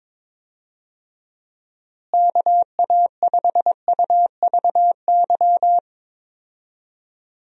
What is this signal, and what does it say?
Morse code, heard as a keyed tone, about 22 wpm, 700 Hz. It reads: KA5UVY
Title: USA Call Signs - 6 Characters - Repeated 2x - 22wpm